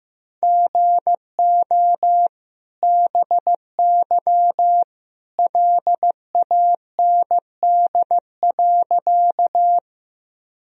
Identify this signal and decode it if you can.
Morse code, heard as a keyed tone, about 15 wpm, 700 Hz. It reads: GO BY LAND.
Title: Sentences from Top 1000 Words - Review Part 1 - 15wpm